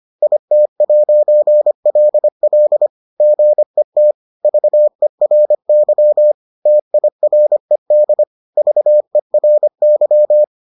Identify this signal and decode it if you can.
Morse code, heard as a keyed tone, about 25 wpm, 600 Hz. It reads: IT'LL GET VERY TIRED VERY